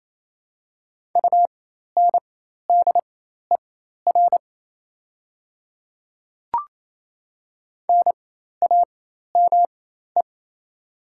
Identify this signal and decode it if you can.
Morse code, heard as a keyed tone, about 27 wpm, spaced as for 13 wpm, 690 Hz. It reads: UNDER NAME
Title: 4x Character Spacing - Top 200 Words - 28wpm